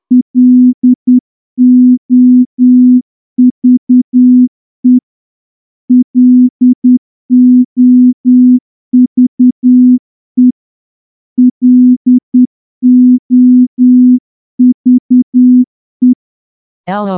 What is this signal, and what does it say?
Morse code, heard as a keyed tone, about 9 wpm, 250 Hz. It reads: LOVE LOVE LOVE
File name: morse-training-10WPM